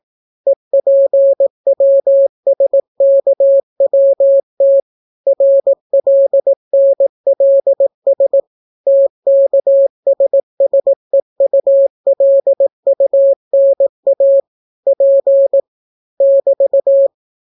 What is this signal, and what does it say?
Morse code, heard as a keyed tone, about 18 wpm, 550 Hz. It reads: EPWSKWT RLNLS TKSSEULUNA P <BT>